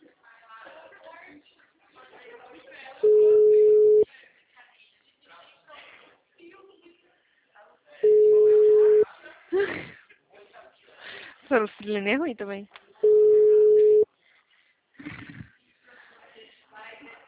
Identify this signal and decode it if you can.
Ringback tone (Europe, 425 Hz), with speech.
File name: out- -2016- - - .39055